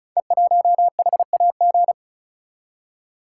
Morse code, heard as a keyed tone, about 33 wpm, 690 Hz. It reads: E1HAG
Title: Code Groups - 5 Letter:Numbers - 35wpm